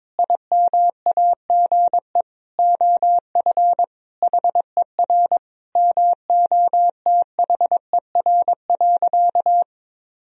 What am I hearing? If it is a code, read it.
Morse code, heard as a keyed tone, about 22 wpm, 700 Hz. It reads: IMAGE OF HER MOTHER.
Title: Sentences from Top 1000 Words - Review Part 4 - 22wpm